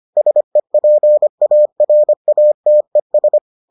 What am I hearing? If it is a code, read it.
Morse code, heard as a keyed tone, about 25 wpm, 600 Hz. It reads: SEPARATES